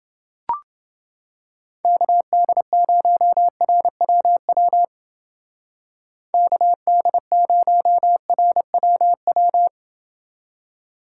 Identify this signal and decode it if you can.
Morse code, heard as a keyed tone, about 28 wpm, 700 Hz. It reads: KD0RWW KD0RWW